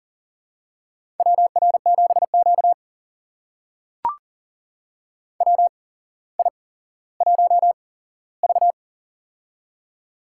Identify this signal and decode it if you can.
Morse code, heard as a keyed tone, about 40 wpm, spaced as for 17 wpm, 700 Hz. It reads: WR7Q WI1V
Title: USA Call Signs - 4 Characters - 8x Character Spacing - 40wpm